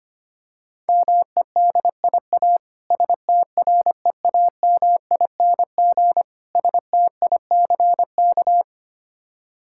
Morse code, heard as a keyed tone, about 26 wpm, 700 Hz. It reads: MEDIA STREAMING STICK
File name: practice / 25wpm / Difficult Words - Speed-Racing with Hints - 25wpm